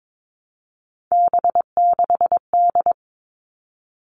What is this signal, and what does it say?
Morse code, heard as a keyed tone, about 22 wpm, 700 Hz. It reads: B6D
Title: All Cut Numbers - 3-Digit 22wpm